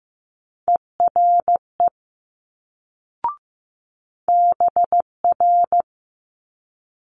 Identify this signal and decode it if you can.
Morse code, heard as a keyed tone, about 15 wpm, 700 Hz. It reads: ERE BR
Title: Multi-letter Phonemes Quick Fire 15wpm